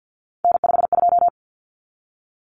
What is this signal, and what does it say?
Morse code, heard as a keyed tone, about 48 wpm, 700 Hz. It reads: N52
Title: Common Cut Numbers - 3-Digit 50wpm